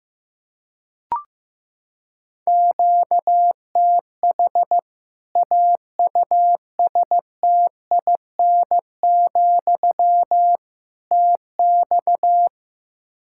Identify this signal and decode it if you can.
Morse code, heard as a keyed tone, about 15 wpm, 700 Hz. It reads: QTH AUSTIN, TX
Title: Practice QSO - Exchange 1 - Beginning - 15wpm